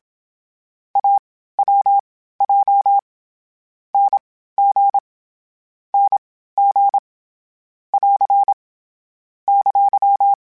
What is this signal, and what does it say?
Morse code, heard as a keyed tone, about 27 wpm, spaced as for 15 wpm, 800 Hz. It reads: AWJ NG NG <AR> !